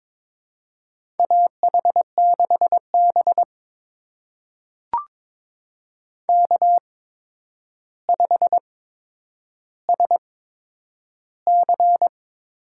Morse code, heard as a keyed tone, about 22 wpm, spaced as for 9 wpm, 700 Hz. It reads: AH6B K5SC
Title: USA Call Signs - 4 Characters - 8x Character Spacing - 22wpm